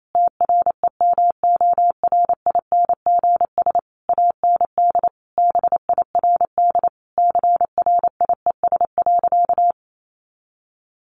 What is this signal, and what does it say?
Morse code, heard as a keyed tone, about 28 wpm, 700 Hz. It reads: TREMORINGS AND BIRD CRIES.
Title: A Sound Of Thunder 28wpm